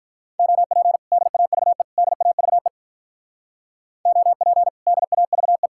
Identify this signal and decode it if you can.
Morse code, heard as a keyed tone, about 45 wpm, 700 Hz. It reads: OP DAVE DAVE OP DAVE